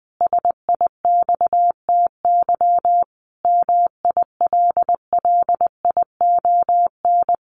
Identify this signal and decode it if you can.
Morse code, heard as a keyed tone, about 20 wpm, 700 Hz. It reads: SIXTY MILLION
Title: A Sound Of Thunder 20wpm